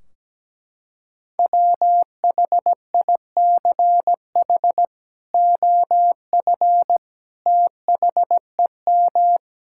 Morse code, heard as a keyed tone, about 17 wpm, 700 Hz. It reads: WHICH OF THEM
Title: Sentences from Top 1600 Words 17wpm